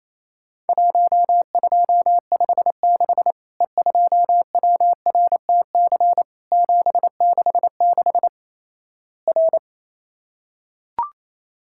Morse code, heard as a keyed tone, about 28 wpm, 700 Hz. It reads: 1256 E2WRTC 766 R
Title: Worked All Europe DX Contest - QTC Exchange - 28wpm